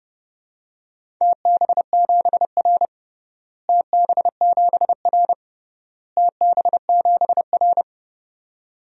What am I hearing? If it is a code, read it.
Morse code, heard as a keyed tone, about 30 wpm, 700 Hz. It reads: TB7R TB7R TB7R
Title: DXCC-100 Call Signs - 4 Characters - Repeated 3x - 30wpm